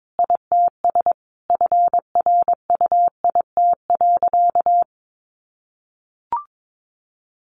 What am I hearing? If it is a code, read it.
Morse code, heard as a keyed tone, about 22 wpm, 700 Hz. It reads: ITS FRUIT.
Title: Sentences from Top 1300 Words 22wpm